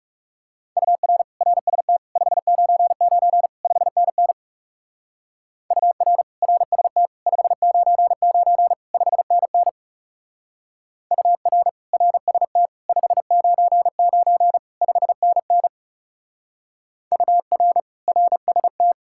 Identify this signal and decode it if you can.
Morse code, heard as a keyed tone, about 37 wpm, 700 Hz. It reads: UR RST 599 5NN UR RST 599 5NN UR RST 599 5NN UR RST
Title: Speed-Racing Common QSO Phrases Encore - 30wpm